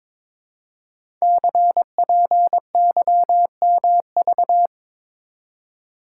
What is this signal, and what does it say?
Morse code, heard as a keyed tone, about 23 wpm, 700 Hz. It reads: CPYMV
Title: Code Groups - 5 Letter:Numbers - 22wpm